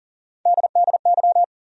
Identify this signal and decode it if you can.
Morse code, heard as a keyed tone, about 40 wpm, 700 Hz. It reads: DDY
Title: USA Call Sign Postfixes - 3 Letters 40wpm